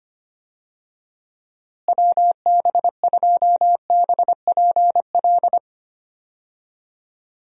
Morse code, heard as a keyed tone, about 25 wpm, 700 Hz. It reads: WB2BPL
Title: USA Call Signs - 6 Characters - Repeated 3x - 25wpm